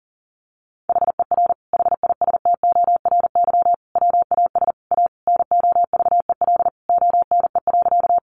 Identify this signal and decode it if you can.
Morse code, heard as a keyed tone, about 40 wpm, 700 Hz. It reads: HER HISTORY WAS A NOVEL ONE.
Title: Sentences from Top 1300 Words 40wpm